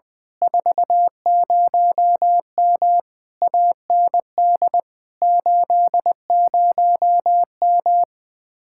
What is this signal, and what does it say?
Morse code, heard as a keyed tone, about 20 wpm, 700 Hz. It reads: 40M AND 80M